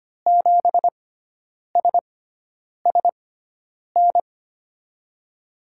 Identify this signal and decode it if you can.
Morse code, heard as a keyed tone, about 25 wpm, spaced as for 9 wpm, 700 Hz. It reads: 7SSN
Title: USA Call Signs - 6 Characters - 6x Character Spacing - 25wpm